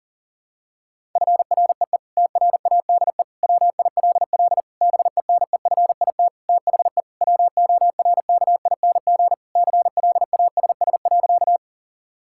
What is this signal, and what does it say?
Morse code, heard as a keyed tone, about 37 wpm, 700 Hz. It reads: FREE TRADE WILL BENEFIT THE WORKING CLASS.
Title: Sentences from Top 1000 Words - Review Part 6 - 40wpm